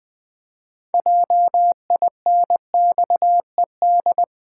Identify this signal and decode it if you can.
Morse code, heard as a keyed tone, about 19 wpm, 700 Hz. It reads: JINXED
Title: Pangram Challenge v2 - 20wpm